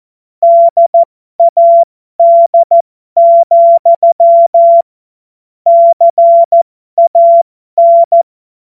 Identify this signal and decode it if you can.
Morse code, heard as a keyed tone, about 14 wpm, spaced as for 12 wpm, 680 Hz. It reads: DAD, CAN
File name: Morsekurs / Witze